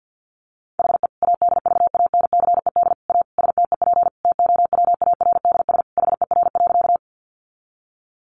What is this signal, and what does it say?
Morse code, heard as a keyed tone, about 50 wpm, 700 Hz. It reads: HE ADVANCED A STEP TOWARDS HER.
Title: Sentences from Top 1200 Words 50wpm